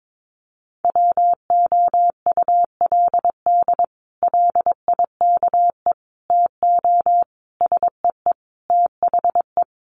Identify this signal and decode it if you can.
Morse code, heard as a keyed tone, about 22 wpm, 700 Hz. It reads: WOULD LIKE TO SEE THE
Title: Sets of 5 Words Encore2 22wpm